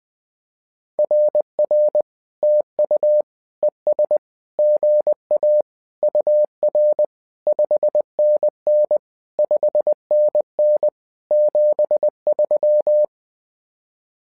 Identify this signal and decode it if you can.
Morse code, heard as a keyed tone, about 20 wpm, 600 Hz. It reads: RR TU ES GA UR 5NN 5NN 73
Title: SOTA QSO Practice 20wpm